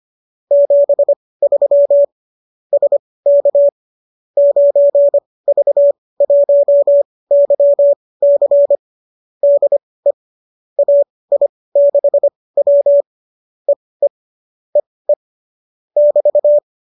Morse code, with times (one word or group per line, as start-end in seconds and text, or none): 0.5-2.0 73
2.7-3.7 SK
4.4-8.8 9V1YC
9.4-10.1 DE
10.8-13.0 AI6W
13.7-14.1 EE
14.8-15.1 EE
16.0-16.6 <BT>